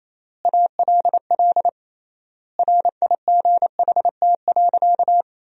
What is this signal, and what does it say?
Morse code, heard as a keyed tone, about 28 wpm, 700 Hz. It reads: ALL RIGHT.
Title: 3x Word-Spacing - Sentences from Top 500 Words - 28wpm